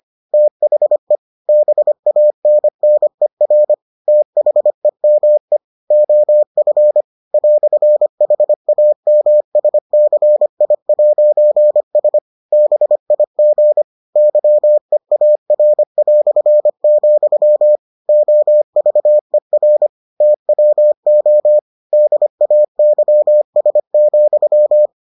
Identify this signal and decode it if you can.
Morse code, heard as a keyed tone, about 25 wpm, 600 Hz. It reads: THE BANNER THEME OF "HAMSCI'S BIG YEAR", OVER TWO DAYS,